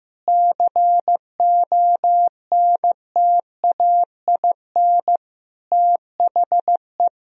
Morse code, heard as a keyed tone, about 16 wpm, 700 Hz. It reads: CONTAIN THE
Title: Sentences from Top 1000 Words - Review Part 2 - 15wpm